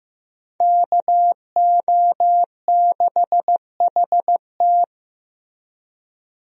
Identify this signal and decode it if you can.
Morse code, heard as a keyed tone, about 15 wpm, 700 Hz. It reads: KO6HT
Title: USA Call Signs - 5 Characters - Repeated 2x - 15wpm